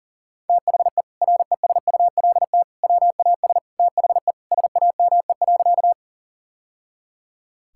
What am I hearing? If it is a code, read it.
Morse code, heard as a keyed tone, about 40 wpm, 700 Hz. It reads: THE RESULT WAS THE SAME.
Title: Sentences from Top 1000 Words - Review Part 5 - 40wpm